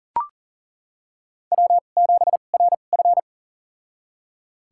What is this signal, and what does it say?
Morse code, heard as a keyed tone, about 39 wpm, spaced as for 29 wpm, 690 Hz. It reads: W7RF
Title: USA Call Signs - 4 Characters - 2x Character Spacing - 40wpm